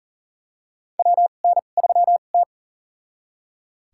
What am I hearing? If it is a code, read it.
Morse code, heard as a keyed tone, about 40 wpm, spaced as for 29 wpm, 690 Hz. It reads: WN3T
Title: USA Call Signs - 4 Characters - 2x Character Spacing - 40wpm